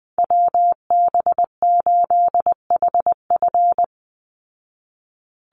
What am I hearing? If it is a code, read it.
Morse code, heard as a keyed tone, about 20 wpm, 700 Hz. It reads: WB8HF
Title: Call Signs Repeated 3x - 20wpm